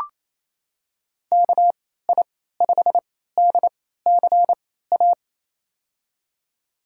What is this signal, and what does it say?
Morse code, heard as a keyed tone, about 28 wpm, spaced as for 16 wpm, 700 Hz. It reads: KI5DCA
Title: USA Call Signs - 6 Characters - 3x Character Spacing - 28wpm